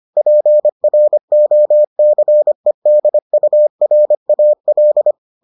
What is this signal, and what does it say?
Morse code, heard as a keyed tone, about 25 wpm, 600 Hz. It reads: PROCEDURAL